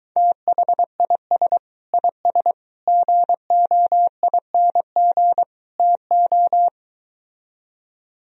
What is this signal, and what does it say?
Morse code, heard as a keyed tone, about 23 wpm, 700 Hz. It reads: THIS IS GOING TO